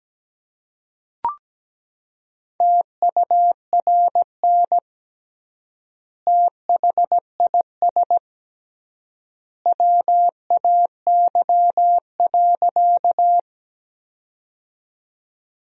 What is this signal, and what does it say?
Morse code, heard as a keyed tone, about 17 wpm, 700 Hz. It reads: TURN THIS WAY.
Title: 3x Word-Spacing - Sentences from Top 200 Words - 17wpm